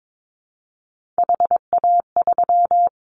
Morse code, heard as a keyed tone, about 22 wpm, 700 Hz. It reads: HA3
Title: Callsign Prefixes for Top 25 DXCC - Repeated 3x - 22wpm